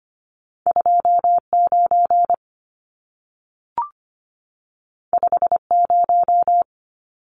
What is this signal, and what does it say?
Morse code, heard as a keyed tone, about 25 wpm, 700 Hz. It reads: 29 50
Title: Numbers - 2-Digit 25wpm